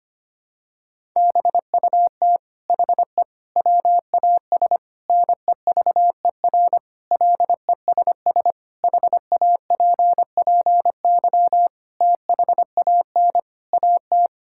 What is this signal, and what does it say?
Morse code, heard as a keyed tone, about 25 wpm, 700 Hz. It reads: BUT HE WAS NEVER LESS HAPPY THAN AT
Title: Sentences from Top 1000 Words - Review Part 5 - 25wpm